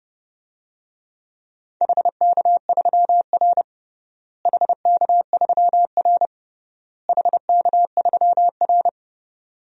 Morse code, heard as a keyed tone, about 30 wpm, 700 Hz. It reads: HK3R HK3R HK3R
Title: DXCC-100 Call Signs - 4 Characters - Repeated 3x - 30wpm